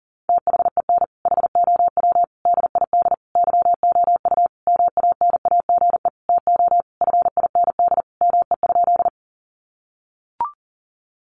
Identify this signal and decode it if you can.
Morse code, heard as a keyed tone, about 40 wpm, 700 Hz. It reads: THEN HOW DID YOU MANAGE TO FIND ME?
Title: Sentences from Top 1400 Words 40wpm